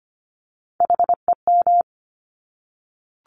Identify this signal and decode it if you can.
Morse code, heard as a keyed tone, about 25 wpm, 700 Hz. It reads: HEM
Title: Top Three Letter Combinations in Words 25wpm